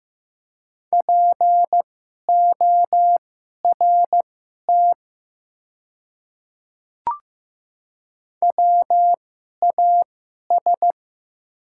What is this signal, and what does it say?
Morse code, heard as a keyed tone, about 15 wpm, spaced as for 11 wpm, 700 Hz. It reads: PORT WAS
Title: 2x Character Spacing - Top 200 Words - 15wpm